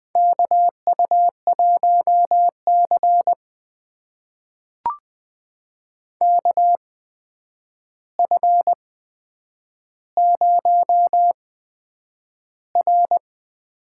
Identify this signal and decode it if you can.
Morse code, heard as a keyed tone, about 20 wpm, spaced as for 9 wpm, 700 Hz. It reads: KU1C KF0R
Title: USA Call Signs - 4 Characters - 8x Character Spacing - 20wpm